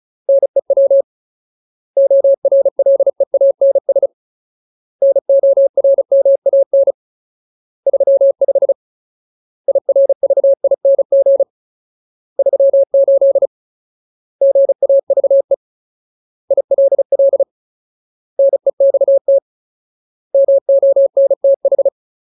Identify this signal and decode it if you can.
Morse code, heard as a keyed tone, about 34 wpm, 550 Hz. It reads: NEW ORLEANS NORMAN 35 IRVING 38 GAVE ILL NEXT MONTH
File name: Ultimate Word List / 35WPM Ultimate